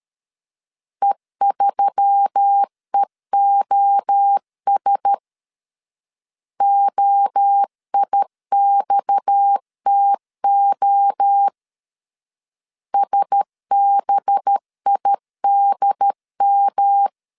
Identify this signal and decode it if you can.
Morse code, heard as a keyed tone, about 13 wpm, 800 Hz. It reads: E3EOS OIXTO SBIDM
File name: CW / DET / 1 6 1500C S B 12wpm 3431